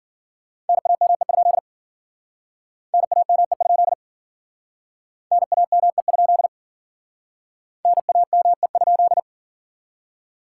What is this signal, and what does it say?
Morse code, heard as a keyed tone, about 50 wpm, 700 Hz. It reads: NAME? NAME? NAME? NAME?
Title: Speed-Racing Common QSO Phrases - 40wpm